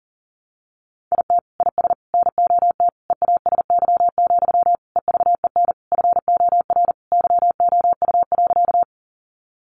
Morse code, heard as a keyed tone, about 40 wpm, 700 Hz. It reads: IT IS NOT EASY, EVEN FOR YOU.
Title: Sentences from Top 700 Words 40wpm